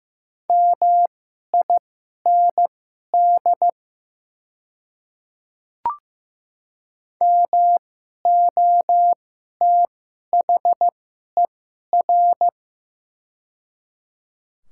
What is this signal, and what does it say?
Morse code, heard as a keyed tone, about 15 wpm, spaced as for 11 wpm, 700 Hz. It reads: MIND MOTHER
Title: 2x Character Spacing - Top 500 Words - 15wpm